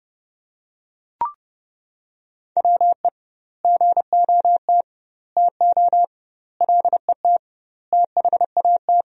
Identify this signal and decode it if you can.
Morse code, heard as a keyed tone, about 32 wpm, 700 Hz. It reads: WE GOT TO LET THAT
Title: 2x Word-Spacing - Sentences from Top 300 Words - 30wpm